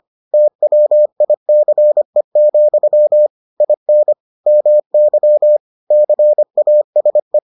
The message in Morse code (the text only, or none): TWICE, IN MY CASE